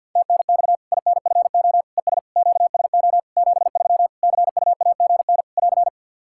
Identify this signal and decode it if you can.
Morse code, heard as a keyed tone, about 52 wpm, 700 Hz. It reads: TNX INFO ES QSO 73 CUAGN <AR>
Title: Practice QSO - Exchange 4 - Ending - 50wpm